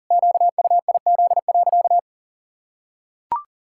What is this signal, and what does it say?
Morse code, heard as a keyed tone, about 37 wpm, 700 Hz. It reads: QUIZ.
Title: Pangram Challenge v2 - 40wpm